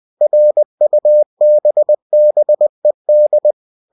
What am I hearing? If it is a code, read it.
Morse code, heard as a keyed tone, about 20 wpm, 600 Hz. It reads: RUBBED